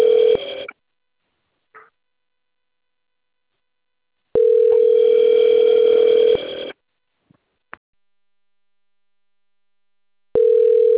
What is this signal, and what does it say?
Ringback tone (North America), 440 + 480 Hz.